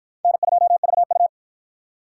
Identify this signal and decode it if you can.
Morse code, heard as a keyed tone, about 53 wpm, 700 Hz. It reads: N2VU